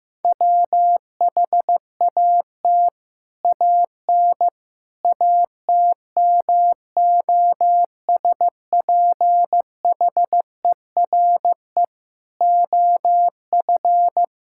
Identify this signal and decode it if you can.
Morse code, heard as a keyed tone, about 16 wpm, 700 Hz. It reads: WHAT AN ATMOSPHERE OF
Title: Sentences from Top 1600 Words 15wpm